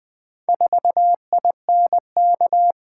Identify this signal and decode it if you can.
Morse code, heard as a keyed tone, about 20 wpm, 700 Hz. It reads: 4INK
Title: Four Letter-Number Combinations - 20wpm